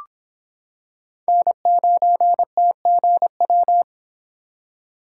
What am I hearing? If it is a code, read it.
Morse code, heard as a keyed tone, about 26 wpm, 700 Hz. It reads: N9TGW